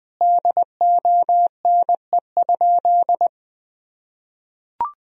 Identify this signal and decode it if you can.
Morse code, heard as a keyed tone, about 21 wpm, 700 Hz. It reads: DONE?
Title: Sentences from Top 500 Words and Abbreviations 20wpm